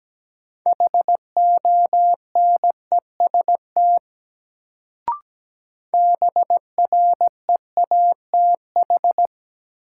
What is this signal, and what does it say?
Morse code, heard as a keyed tone, about 17 wpm, 700 Hz. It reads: HONEST BREATH
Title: Sets of 5 Words Encore - New Word Drill - 17wpm